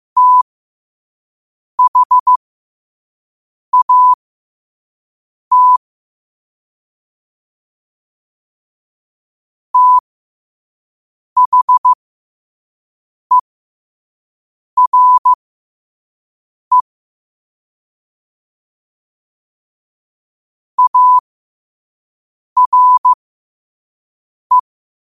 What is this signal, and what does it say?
Morse code, heard as a keyed tone, about 15 wpm, spaced as for 5 wpm, 1000 Hz. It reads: THAT THERE ARE
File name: qotdfiles / 5wpm